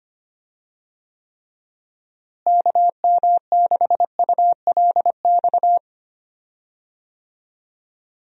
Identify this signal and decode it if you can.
Morse code, heard as a keyed tone, about 25 wpm, 700 Hz. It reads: KM6ULX
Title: USA Call Signs - 6 Characters - Repeated 2x - 25wpm